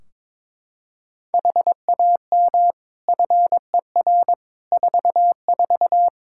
Morse code, heard as a keyed tone, about 22 wpm, 700 Hz. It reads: HAM FER 44
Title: K1USN SST QSO Practice 22wpm